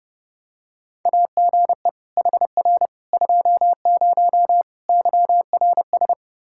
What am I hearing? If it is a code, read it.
Morse code, heard as a keyed tone, about 30 wpm, 700 Hz. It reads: AGE HR 20 YRS
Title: K1USN SST QSO Practice 30wpm